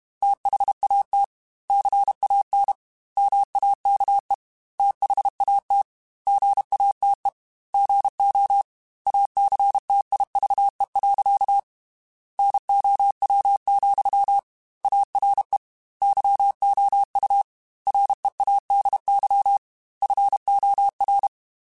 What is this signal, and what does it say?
Morse code, heard as a keyed tone, about 32 wpm, 800 Hz. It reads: THAT CAN MAKE THAT GATE GO ACTIVE. NOW, ARE YOU READY FOR
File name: Technical Subjects / Technical Subjects-mp3s / OR Gate @30